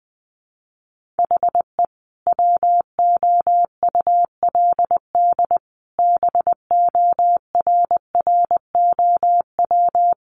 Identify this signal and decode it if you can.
Morse code, heard as a keyed tone, about 20 wpm, 700 Hz. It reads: HE WOULD BORROW
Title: Sentences from Top 1500 Words 20wpm